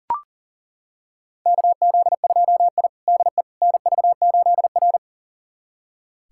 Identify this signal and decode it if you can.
Morse code, heard as a keyed tone, about 40 wpm, 700 Hz. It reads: KZ2I DE NV8R
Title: USA Callsign DE Callsign - 4 Characters - 40wpm